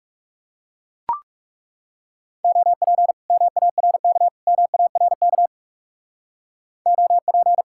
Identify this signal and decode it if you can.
Morse code, heard as a keyed tone, about 36 wpm, 700 Hz. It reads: OP MARK MARK OP
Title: Speed-Racing Common QSO Phrases Encore - 30wpm